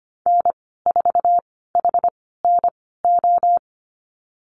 Morse code, heard as a keyed tone, about 25 wpm, spaced as for 16 wpm, 700 Hz. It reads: N4HNO
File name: Lesson 15g - Number 5 - Call Signs 25wpm